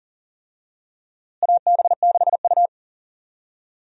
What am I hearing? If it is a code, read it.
Morse code, heard as a keyed tone, about 40 wpm, 700 Hz. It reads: AB6U
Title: USA Call Signs - 4 Characters - Repeated 3x - 40wpm